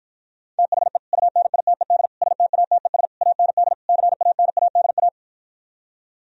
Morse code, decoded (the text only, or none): THE UNITED STATES AND CANADA